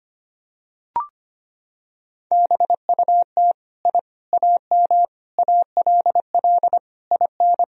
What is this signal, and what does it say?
Morse code, heard as a keyed tone, about 25 wpm, 700 Hz. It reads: BUT I AM ALL IN
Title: Sentences from Top 1000 Words - Review Part 2 - 25wpm